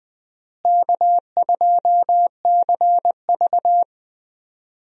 Morse code, heard as a keyed tone, about 20 wpm, 700 Hz. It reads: K2CV
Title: USA Call Signs - 4 Characters - Repeated 3x - 20wpm